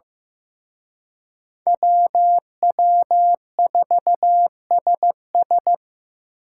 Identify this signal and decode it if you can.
Morse code, heard as a keyed tone, about 18 wpm, 700 Hz. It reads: WW4SS
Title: Speed Racing - USA 2x2 Call Signs 15wpm